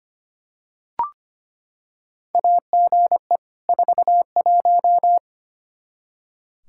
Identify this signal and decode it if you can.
Morse code, heard as a keyed tone, about 25 wpm, 700 Hz. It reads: AGE 41
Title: Practice QSO - Exchange 3 - More Info - 25wpm